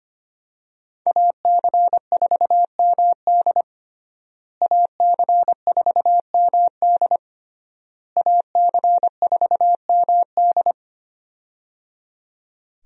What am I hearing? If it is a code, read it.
Morse code, heard as a keyed tone, about 25 wpm, 700 Hz. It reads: AC4MD AC4MD AC4MD
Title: USA Call Signs - 5 Characters - Repeated 3x - 25wpm